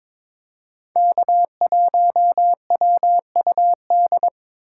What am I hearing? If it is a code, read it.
Morse code, heard as a keyed tone, about 22 wpm, 700 Hz. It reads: K1WUD
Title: USA Call Signs - 6x Character Spacing - 22wpm